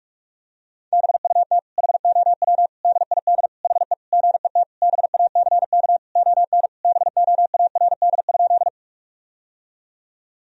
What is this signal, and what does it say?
Morse code, heard as a keyed tone, about 46 wpm, 700 Hz. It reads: BUT HOW DID HE GET BACK ON BOARD?
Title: Sentences from Top 1000 Words - Review Part 4 - 45wpm